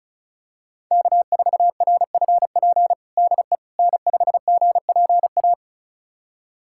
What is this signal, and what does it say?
Morse code, heard as a keyed tone, about 36 wpm, 700 Hz. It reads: K4RFP DE N5GPA
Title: USA Callsign DE Callsign - 5 Characters - 35wpm